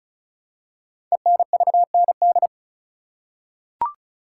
Morse code, heard as a keyed tone, about 36 wpm, 700 Hz. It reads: ENVND
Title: Code Groups - 5 Letters - 35wpm